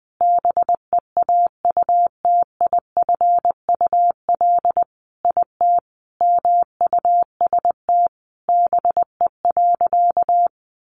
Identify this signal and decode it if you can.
Morse code, heard as a keyed tone, about 20 wpm, 700 Hz. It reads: BEAUTIFUL IT MUST BE.
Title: Sentences from Top 900 Words 20wpm